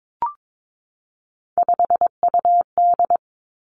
Morse code, heard as a keyed tone, about 22 wpm, 700 Hz. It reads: 5UD
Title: Three Challenging Letter-Number Combinations 22wpm